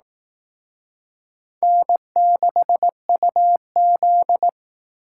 Morse code, heard as a keyed tone, about 19 wpm, 700 Hz. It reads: N6UZ